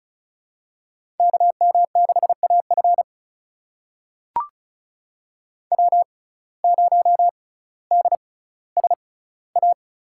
Morse code, heard as a keyed tone, about 36 wpm, spaced as for 18 wpm, 700 Hz. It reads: KM6AF W0DSA
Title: USA Call Signs - 5 Characters - 6x Character Spacing - 35wpm